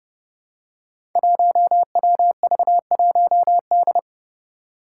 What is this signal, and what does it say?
Morse code, heard as a keyed tone, about 32 wpm, 700 Hz. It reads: 1WV1D
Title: Five Challenging Letter-Number Combinations 30wpm